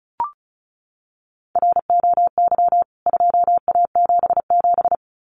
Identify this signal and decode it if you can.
Morse code, heard as a keyed tone, about 34 wpm, 700 Hz. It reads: ROY 2A77
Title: CWT Exchange 35wpm